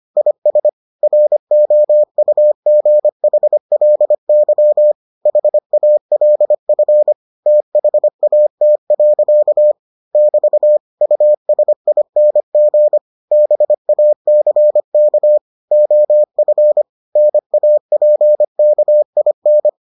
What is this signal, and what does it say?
Morse code, heard as a keyed tone, about 25 wpm, 600 Hz. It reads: IS ROUGHLY HALF THAT. <BT> USING BACK OF NAPKIN